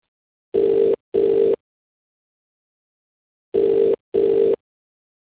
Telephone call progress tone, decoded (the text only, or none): ringback tone